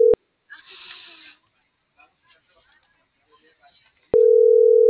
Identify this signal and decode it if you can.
Ringback tone (North America), 440 + 480 Hz.